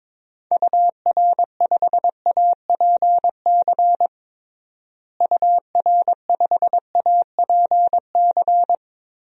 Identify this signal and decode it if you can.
Morse code, heard as a keyed tone, about 22 wpm, 700 Hz. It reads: UR5APC UR5APC
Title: DXCC-100 Call Signs - 6 Characters - Repeated 2x - 22wpm